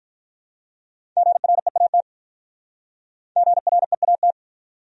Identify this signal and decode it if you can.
Morse code, heard as a keyed tone, about 48 wpm, 700 Hz. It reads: GREAT GREAT